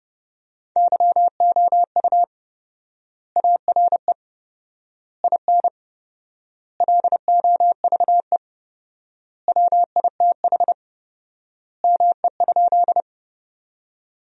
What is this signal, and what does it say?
Morse code, heard as a keyed tone, about 32 wpm, 700 Hz. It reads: YOU ARE IN LOVE WITH ME?
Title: 4x Word-Spacing - Sentences from Top 400 Words - 30wpm